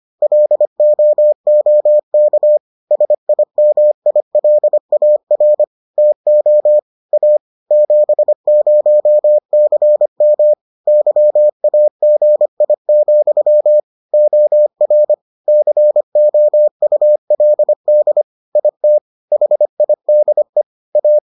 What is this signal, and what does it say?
Morse code, heard as a keyed tone, about 25 wpm, 600 Hz. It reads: LOOK SIMILAR TO A 70CM YAGI, OR COULD IT HIDE A